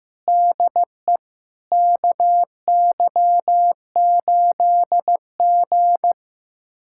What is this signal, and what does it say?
Morse code, heard as a keyed tone, about 15 wpm, 700 Hz. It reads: DE KY8G
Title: USA Callsign DE Callsign - 4 Characters - 15wpm